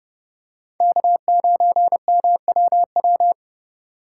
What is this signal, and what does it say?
Morse code, heard as a keyed tone, about 29 wpm, 700 Hz. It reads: K9MWW